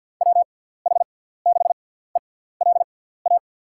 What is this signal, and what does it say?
Morse code, heard as a keyed tone, about 53 wpm, spaced as for 17 wpm, 700 Hz. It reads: WH6ELA
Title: USA Call Signs - 6 Characters - 6x Character Spacing - 50wpm